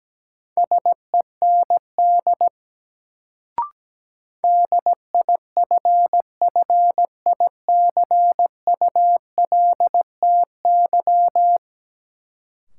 Morse code, heard as a keyed tone, about 17 wpm, 700 Hz. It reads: SEND DIFFICULTY
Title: Sentences from Top 1500 Words - New Word Drill - 17wpm